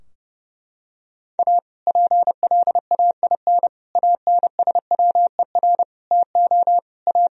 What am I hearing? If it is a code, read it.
Morse code, heard as a keyed tone, about 32 wpm, 700 Hz. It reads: A PLAIN ANSWER TO A
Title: Sentences from Top 1000 Words - Review Part 2 - 30wpm